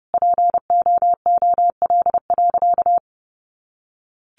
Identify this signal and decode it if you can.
Morse code, heard as a keyed tone, about 29 wpm, 700 Hz. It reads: POOL.
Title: Easy Phrases 30wpm